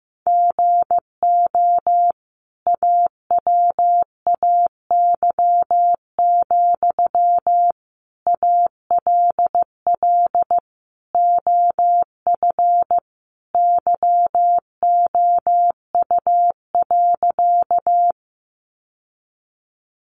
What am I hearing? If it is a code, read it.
Morse code, heard as a keyed tone, about 15 wpm, 700 Hz. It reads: GO AWAY, ALL OF YOU.
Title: Sentences from Top 600 Words 15wpm